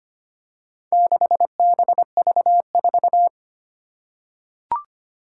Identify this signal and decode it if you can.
Morse code, heard as a keyed tone, about 25 wpm, 700 Hz. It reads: 6BV4
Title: Four Challenging Letter-Number Combinations 25wpm